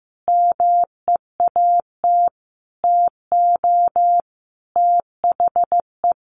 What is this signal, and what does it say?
Morse code, heard as a keyed tone, about 15 wpm, 700 Hz. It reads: MEAT TO THE
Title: Sentences from Top 1500 Words 15wpm